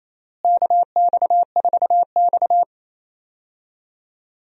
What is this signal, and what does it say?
Morse code, heard as a keyed tone, about 27 wpm, 700 Hz. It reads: KX4X
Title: USA Call Signs - 4 Characters - Repeated 3x - 28wpm